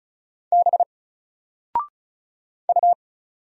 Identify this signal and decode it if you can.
Morse code, heard as a keyed tone, about 34 wpm, 700 Hz. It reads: B U
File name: Single-Letter-Number-35wpm